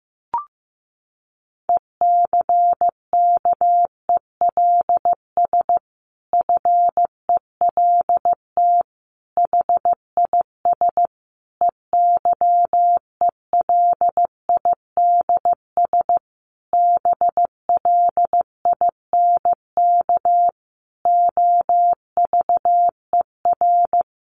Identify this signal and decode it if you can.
Morse code, heard as a keyed tone, about 15 wpm, 700 Hz. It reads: ECKELS FELT HIS EYELIDS BLINK OVER
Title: A Sound Of Thunder 15wpm